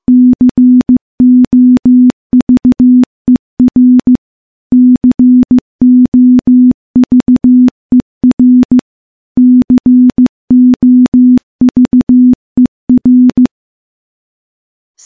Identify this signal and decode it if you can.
Morse code, heard as a keyed tone, about 15 wpm, 250 Hz. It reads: COVER COVER COVER
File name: morse / morse-training-15WPM